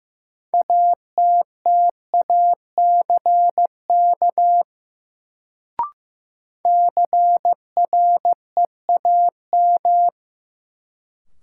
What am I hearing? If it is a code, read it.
Morse code, heard as a keyed tone, about 15 wpm, 700 Hz. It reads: ATTACK CREAM
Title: Sentences from Top 1500 Words - New Word Drill - 15wpm